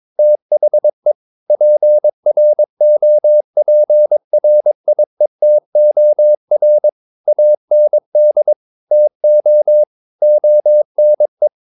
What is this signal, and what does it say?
Morse code, heard as a keyed tone, about 22 wpm, 600 Hz. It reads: THE PROPRIETOR AND TO ONE
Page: Romanen Never af Ken Follet på engelsk.